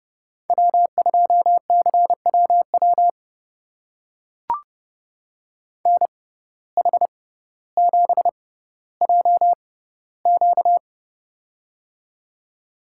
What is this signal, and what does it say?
Morse code, heard as a keyed tone, about 31 wpm, spaced as for 16 wpm, 700 Hz. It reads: W2CWW NH7JQ
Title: USA Call Signs - 5 Characters - 6x Character Spacing - 30wpm